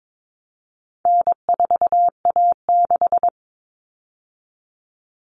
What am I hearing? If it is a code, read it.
Morse code, heard as a keyed tone, about 22 wpm, 700 Hz. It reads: N4A6
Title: All Cut Numbers - 4-Digit 22wpm